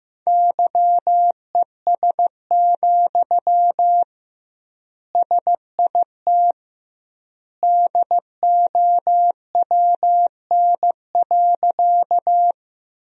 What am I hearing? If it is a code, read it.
Morse code, heard as a keyed tone, about 16 wpm, 700 Hz. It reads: YES, SIT DOWN.
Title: 2x Word-Spacing - Sentences from Top 500 Words - 15wpm